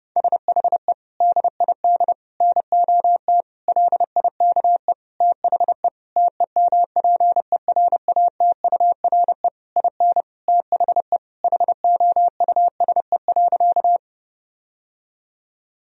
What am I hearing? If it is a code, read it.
Morse code, heard as a keyed tone, about 32 wpm, 700 Hz. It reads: SHE DID NOT LIKE THE TEMPERATURE IN THE HOUSE.
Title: Sentences from Top 1000 Words - Review Part 4 - 30wpm